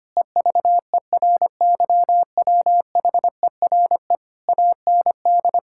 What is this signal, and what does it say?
Morse code, heard as a keyed tone, about 25 wpm, 700 Hz. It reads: EVERYWHERE AND
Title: Sentences from Top 1600 Words 25wpm